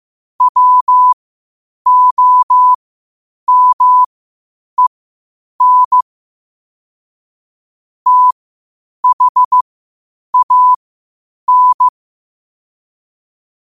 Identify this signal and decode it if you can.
Morse code, heard as a keyed tone, about 15 wpm, spaced as for 8 wpm, 1000 Hz. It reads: WOMEN THAN
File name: qotdfiles / 8wpm